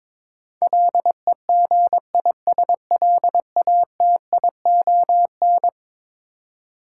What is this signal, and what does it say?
Morse code, heard as a keyed tone, about 22 wpm, 700 Hz. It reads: LEGISLATION
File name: practice / 22wpm / Eleven Letter Words - 22wpm